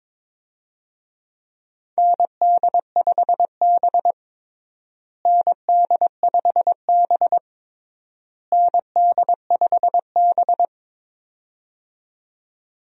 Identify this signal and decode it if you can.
Morse code, heard as a keyed tone, about 22 wpm, 700 Hz. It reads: ND5B ND5B ND5B
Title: USA Call Signs - 4 Characters - Repeated 3x - 22wpm